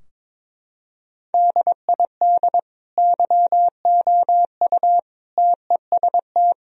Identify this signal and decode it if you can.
Morse code, heard as a keyed tone, about 22 wpm, 700 Hz. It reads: DID YOU TEST
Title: Sentences from Top 1000 Words - Review Part 4 - 22wpm